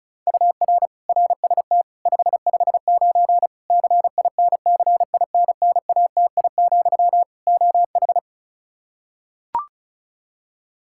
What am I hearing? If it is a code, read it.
Morse code, heard as a keyed tone, about 36 wpm, 700 Hz. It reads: UR RST 559 CINCINNATI, OH
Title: Practice QSO - Exchange 1 - Beginning - 35wpm